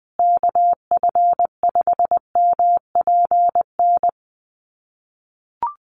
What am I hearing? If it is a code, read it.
Morse code, heard as a keyed tone, about 20 wpm, 700 Hz. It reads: KF5MPN
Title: USA Call Signs Repeated 2x 20wpm